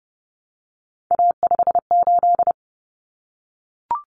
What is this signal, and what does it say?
Morse code, heard as a keyed tone, about 30 wpm, 700 Hz. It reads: A58
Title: Common Cut Numbers - 3-Digit 30wpm